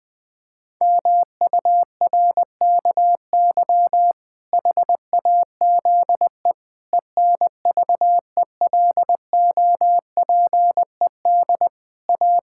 Morse code, heard as a keyed tone, about 20 wpm, 700 Hz. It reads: MURKY HAZE ENVELOPED A
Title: Pangram Challenge v2 - 20wpm